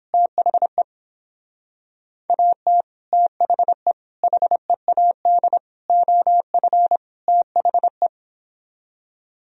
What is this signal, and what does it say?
Morse code, heard as a keyed tone, about 26 wpm, 700 Hz. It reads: THE AT THE HEAD OF THE
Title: Speed-Racing - Sets of 5 Words 22wpm